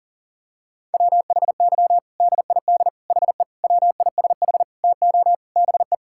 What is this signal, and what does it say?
Morse code, heard as a keyed tone, about 40 wpm, 700 Hz. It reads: WHY DID HE WISH TO BE
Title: Sentences from Top 1600 Words 40wpm